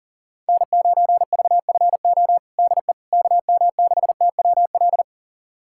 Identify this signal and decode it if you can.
Morse code, heard as a keyed tone, about 40 wpm, 700 Hz. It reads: N9VFO DE KM6TWL
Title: USA Callsign DE Callsign 40wpm